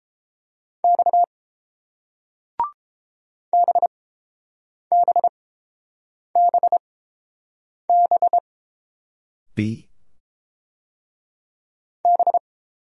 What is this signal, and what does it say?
Morse code, heard as a keyed tone, about 29 wpm, spaced as for 7 wpm, 700 Hz. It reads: X BBBB B